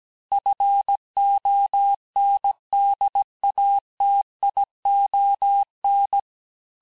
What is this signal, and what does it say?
Morse code, heard as a keyed tone, about 17 wpm, 800 Hz. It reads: FONDATION
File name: texte clair vitesse 16 puis 21 mpm